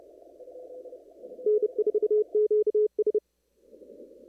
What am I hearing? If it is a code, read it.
Morse code, heard as a keyed tone, about 30 wpm, 420 Hz. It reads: N4QS
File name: CW / kt5j wpx cw-op-k5pi